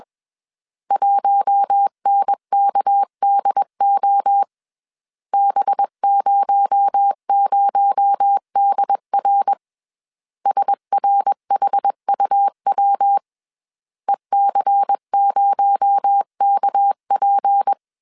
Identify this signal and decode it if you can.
Morse code, heard as a keyed tone, about 21 wpm, 800 Hz. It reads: 1NKDO B00DR SRHUW EC0KP
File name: CW / DET / 1 6 1500C S B 20wpm 2048